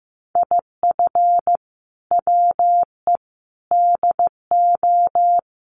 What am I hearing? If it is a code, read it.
Morse code, heard as a keyed tone, about 15 wpm, 700 Hz. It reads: IF WE DO
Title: Sets of 4 Words 15wpm